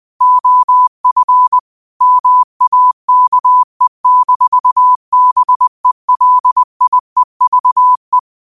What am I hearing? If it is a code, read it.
Morse code, heard as a keyed tone, about 20 wpm, 1000 Hz. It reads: OF MAKE-BELIEVE
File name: qotdfiles / 20wpm